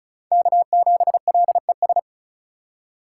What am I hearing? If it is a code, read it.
Morse code, heard as a keyed tone, about 36 wpm, 700 Hz. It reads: K7LES
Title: USA Call Signs - 5 Characters - 3x Character Spacing - 35wpm